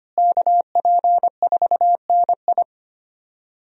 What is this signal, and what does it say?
Morse code, heard as a keyed tone, about 25 wpm, 700 Hz. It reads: KP4NI
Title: USA Call Signs - 5 Characters - Repeated 3x - 25wpm